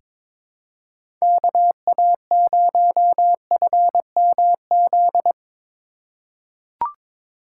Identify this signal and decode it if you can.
Morse code, heard as a keyed tone, about 21 wpm, 700 Hz. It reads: KA0FMZ
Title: USA Call Signs - 6 Characters - 4x Character Spacing - 22wpm